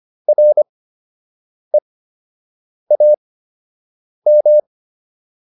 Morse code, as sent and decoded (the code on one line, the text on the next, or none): .-. .- --
REAM